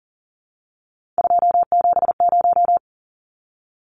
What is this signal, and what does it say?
Morse code, heard as a keyed tone, about 40 wpm, 700 Hz. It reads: 270
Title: Numbers - 3-Digit 40wpm